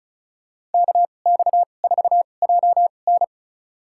Morse code, heard as a keyed tone, about 34 wpm, spaced as for 25 wpm, 700 Hz. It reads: KX4JN
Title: USA Call Signs - 5 Characters - 2x Character Spacing - 35wpm